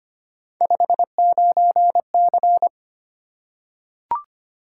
Morse code, heard as a keyed tone, about 25 wpm, 700 Hz. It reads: 59C
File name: Three Letter-Number Combinations - 25wpm